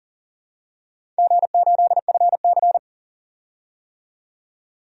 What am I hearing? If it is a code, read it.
Morse code, heard as a keyed tone, about 39 wpm, 700 Hz. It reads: G8FC
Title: DXCC-100 Call Signs - 4 Characters - 40wpm